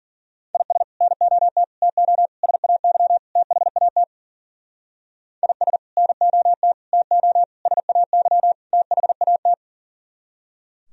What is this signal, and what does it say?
Morse code, heard as a keyed tone, about 51 wpm, 700 Hz. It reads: IS NOT TO SAY THAT IS NOT TO SAY THAT